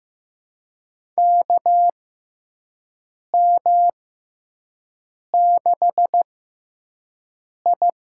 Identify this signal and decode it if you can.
Morse code, heard as a keyed tone, about 15 wpm, spaced as for 5 wpm, 700 Hz. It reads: KM6I
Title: USA Call Signs - 6 Characters - 6x Character Spacing - 15wpm